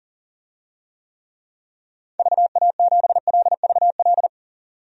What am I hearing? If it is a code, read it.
Morse code, heard as a keyed tone, about 40 wpm, 700 Hz. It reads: VA7LVL
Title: DXCC-100 Call Signs - 6 Characters - 40wpm